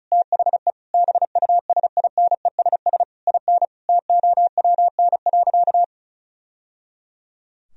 Morse code, heard as a keyed tone, about 36 wpm, 700 Hz. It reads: THE BUSINESS IN TOWN.
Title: Sentences from Top 1000 Words - Review Part 2 - 35wpm